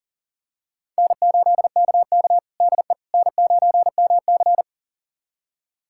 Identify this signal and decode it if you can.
Morse code, heard as a keyed tone, about 40 wpm, 700 Hz. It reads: N8KK DE N9MC
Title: USA Callsign DE Callsign - 4 Characters - 40wpm